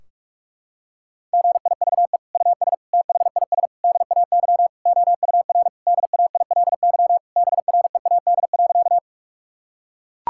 Morse code, heard as a keyed tone, about 42 wpm, 700 Hz. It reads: GIVE US THIS DAY OUR DAILY BREAD.
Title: Sentences from Top 1600 Words 45wpm